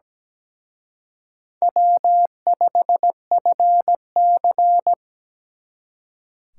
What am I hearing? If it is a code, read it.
Morse code, heard as a keyed tone, about 20 wpm, 700 Hz. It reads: W5FC
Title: Speed Racing - USA 1x2 Call Signs 17wpm